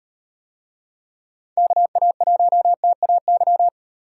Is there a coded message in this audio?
Yes: Morse code from 1.6 to 3.7 s, heard as a keyed tone, about 36 wpm, 700 Hz. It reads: KA1TAY